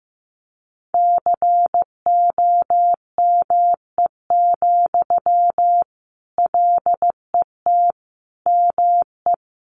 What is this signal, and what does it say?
Morse code, heard as a keyed tone, about 15 wpm, 700 Hz. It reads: COME, LET ME
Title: Sentences from Top 500 Words 15wpm